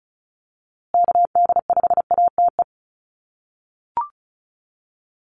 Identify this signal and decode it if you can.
Morse code, heard as a keyed tone, about 34 wpm, 700 Hz. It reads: KD5ATE
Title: USA Call Signs Repeated 2x 35wpm